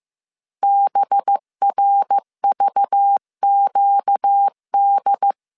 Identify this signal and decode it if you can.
Morse code, heard as a keyed tone, about 15 wpm, 800 Hz. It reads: BRVQD